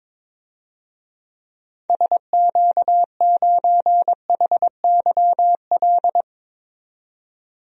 Morse code, heard as a keyed tone, about 22 wpm, 700 Hz. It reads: SQ9HYL
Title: DXCC-100 Call Signs - 6 Characters - 22wpm